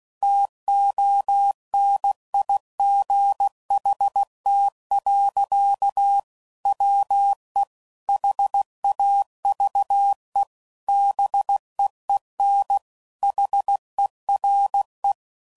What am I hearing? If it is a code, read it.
Morse code, heard as a keyed tone, about 16 wpm, 800 Hz. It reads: TONIGHT. WE HAVE BEEN HERE
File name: CW Code Groups @15WPM / QSOs / WD0BGP Spaced